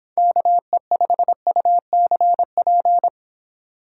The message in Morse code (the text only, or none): KE5UCP